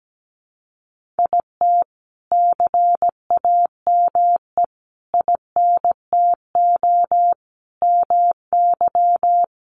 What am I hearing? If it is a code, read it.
Morse code, heard as a keyed tone, about 17 wpm, 700 Hz. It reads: IT CAME INTO MY
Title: Sentences from Top 1500 Words 17wpm